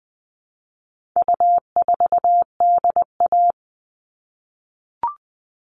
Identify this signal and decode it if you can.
Morse code, heard as a keyed tone, about 20 wpm, 700 Hz. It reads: U4DA
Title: All Cut Numbers - 4-Digit 20wpm